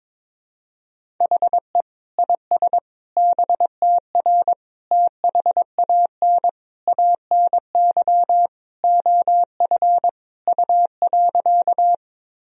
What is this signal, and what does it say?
Morse code, heard as a keyed tone, about 22 wpm, 700 Hz. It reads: HE IS BTR THAN ANY OF U.
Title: Sentences from Top 400 Words and Abbreviations 22wpm